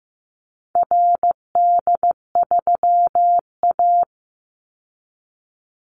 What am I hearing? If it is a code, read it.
Morse code, heard as a keyed tone, about 15 wpm, 700 Hz. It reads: RD3A
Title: Call Signs Repeated 2x - 15wpm